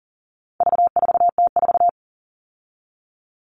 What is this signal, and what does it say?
Morse code, heard as a keyed tone, about 40 wpm, 700 Hz. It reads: V4T4
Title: All Cut Numbers - 4-Digit 40wpm